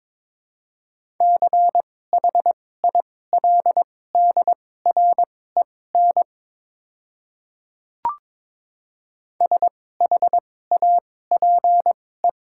Morse code, heard as a keyed tone, about 22 wpm, spaced as for 16 wpm, 700 Hz. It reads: CHILDREN SHAPE
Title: 2x Character Spacing - Top 500 Words - 22wpm